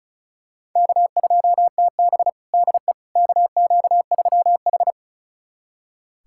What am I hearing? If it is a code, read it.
Morse code, heard as a keyed tone, about 34 wpm, 700 Hz. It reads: K2TB DE KQ3H
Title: USA Callsign DE Callsign - 4 Characters - 35wpm